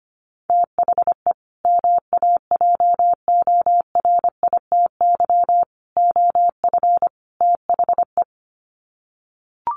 Morse code, heard as a keyed tone, about 25 wpm, 700 Hz. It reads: THE MAJORITY OF THE
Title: Sets of 4 Words 25wpm